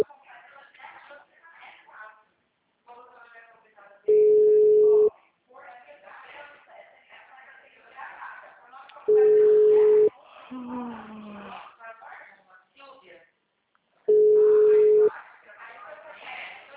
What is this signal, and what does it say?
Ringback tone (Europe, 425 Hz), with speech.
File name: out- -2024- - - .81696